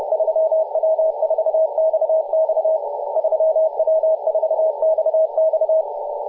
Morse code, heard as a keyed tone, about 30 wpm, 640 Hz, with radio noise. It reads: 3W4XX 3W4XX
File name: posts / 2016-aug-17-0039z-3w4xx-14-022mhz-cw